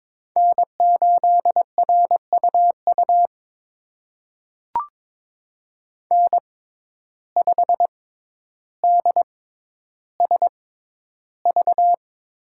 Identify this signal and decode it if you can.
Morse code, heard as a keyed tone, about 22 wpm, spaced as for 11 wpm, 700 Hz. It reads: N8RUU N5DSV
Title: USA Call Signs - 6x Character Spacing - 22wpm